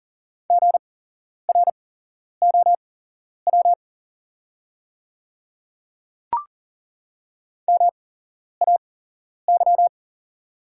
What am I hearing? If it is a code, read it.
Morse code, heard as a keyed tone, about 40 wpm, spaced as for 11 wpm, 690 Hz. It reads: GROW MAY
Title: 8x Character Spacing - Top 300 Words - 40wpm